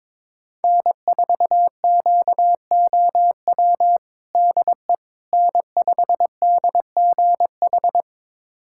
Morse code, heard as a keyed tone, about 22 wpm, 700 Hz. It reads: N4QOW DE N5DGH
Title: USA Callsign DE Callsign - 5 Characters - 22wpm